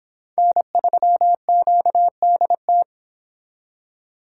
Morse code, heard as a keyed tone, about 30 wpm, 700 Hz. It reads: N3QDT